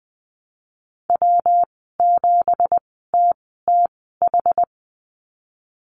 Morse code, heard as a keyed tone, about 20 wpm, spaced as for 15 wpm, 700 Hz. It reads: W7TTH
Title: 173 Lesson 28g - Number 9 - Call Signs 20wpm